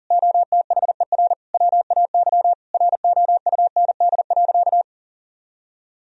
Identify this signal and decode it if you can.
Morse code, heard as a keyed tone, about 40 wpm, 700 Hz. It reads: OTHER WAY ROUND.
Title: Sentences from Top 1000 Words - Review Part 3 - 40wpm